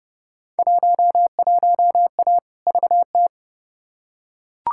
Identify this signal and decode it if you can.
Morse code, heard as a keyed tone, about 30 wpm, 700 Hz. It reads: 11A VT
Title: Field Day Exchanges Encore - 30wpm